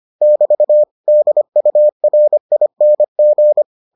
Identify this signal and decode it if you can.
Morse code, heard as a keyed tone, about 25 wpm, 600 Hz. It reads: <BT> DURING